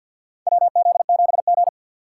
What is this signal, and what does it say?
Morse code, heard as a keyed tone, about 53 wpm, 700 Hz. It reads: WZ7G
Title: USA Call Signs - 4 Characters - 2x Character Spacing - 50wpm